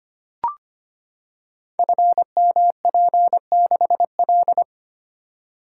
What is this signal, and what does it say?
Morse code, heard as a keyed tone, about 25 wpm, 700 Hz. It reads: FMP6L
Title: Code Groups - 5 Letter:Numbers - 25wpm